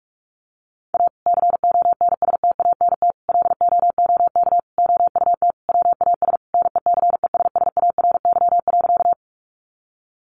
Morse code, heard as a keyed tone, about 44 wpm, 700 Hz. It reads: A CONSTANT LOOK OUT WAS NECESSARY.
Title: Sentences from Top 1500 Words 45wpm